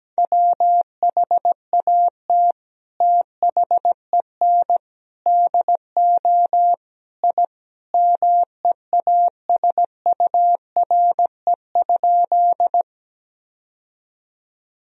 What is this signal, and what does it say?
Morse code, heard as a keyed tone, about 17 wpm, 700 Hz. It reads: WHAT THEN DO I MEASURE?
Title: Sentences from Top 1000 Words - Review Part 2 - 17wpm